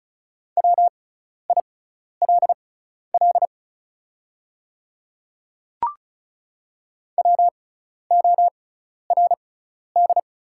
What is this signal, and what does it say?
Morse code, heard as a keyed tone, about 36 wpm, spaced as for 12 wpm, 700 Hz. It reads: WILL WORD
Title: 6x Character Spacing - Top 300 Words - 35wpm